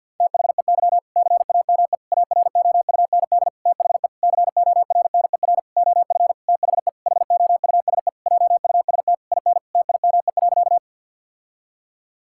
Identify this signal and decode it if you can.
Morse code, heard as a keyed tone, about 50 wpm, 700 Hz. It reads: THEY CAME AROUND THE CORNER OF THE HOUSE JUST IN TIME.
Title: Sentences from Top 1600 Words 50wpm